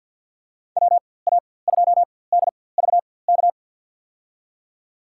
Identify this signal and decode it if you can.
Morse code, heard as a keyed tone, about 50 wpm, spaced as for 23 wpm, 700 Hz. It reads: WA2DVK
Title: USA Call Signs - 6 Characters - 4x Character Spacing - 50wpm